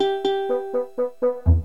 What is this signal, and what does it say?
There is no code.